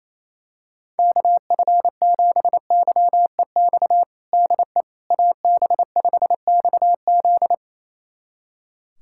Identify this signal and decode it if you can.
Morse code, heard as a keyed tone, about 28 wpm, 700 Hz. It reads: KF7YEX DE AB5XZ
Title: USA Callsign DE Callsign 28wpm